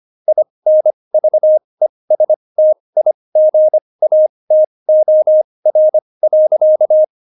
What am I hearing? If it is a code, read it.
Morse code, heard as a keyed tone, about 25 wpm, spaced as for 20 wpm, 620 Hz. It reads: INVESTIGATOR.
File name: Jokes-25-char-20-eff